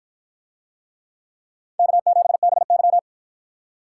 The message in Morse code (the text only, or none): K7BY